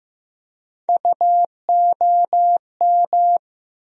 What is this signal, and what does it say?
Morse code, heard as a keyed tone, about 15 wpm, 700 Hz. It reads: UOM
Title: USA Call Sign Postfixes - 3 Letters 15wpm